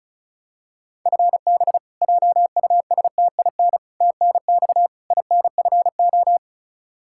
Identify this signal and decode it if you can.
Morse code, heard as a keyed tone, about 34 wpm, 700 Hz. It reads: FB JUSTIN TNX INFO
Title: Practice QSO - Exchange 3 - More Info - 35wpm